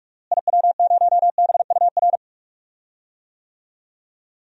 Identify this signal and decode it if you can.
Morse code, heard as a keyed tone, about 46 wpm, 700 Hz. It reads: IW0BUR
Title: DXCC-100 Call Signs - 6 Characters - Repeated 3x - 45wpm